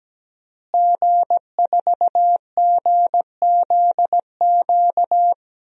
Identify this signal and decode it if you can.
Morse code, heard as a keyed tone, about 17 wpm, 700 Hz. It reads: G4GZQ
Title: DXCC-100 Call Signs - 5 Characters - Repeated 3x - 17wpm